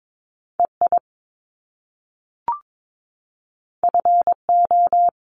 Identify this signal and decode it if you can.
Morse code, heard as a keyed tone, about 22 wpm, 700 Hz. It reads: EI FO
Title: Top Two Letter Combinations in Words 22wpm